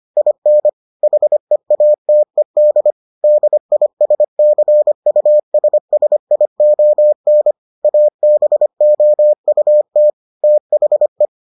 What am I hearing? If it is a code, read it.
Morse code, heard as a keyed tone, about 25 wpm, 600 Hz. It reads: IN HEATED DISCUSSION ABOUT THE